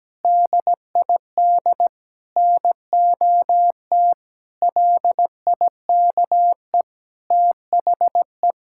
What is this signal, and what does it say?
Morse code, heard as a keyed tone, about 17 wpm, 700 Hz. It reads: DID NOT LIKE THE
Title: Sentences from Top 1000 Words - Review Part 6 - 17wpm